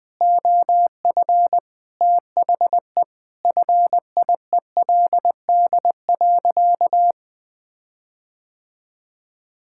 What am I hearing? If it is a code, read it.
Morse code, heard as a keyed tone, about 20 wpm, 700 Hz. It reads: OF THE FIELD.
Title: Sentences with 5-Word Phrases - 20wpm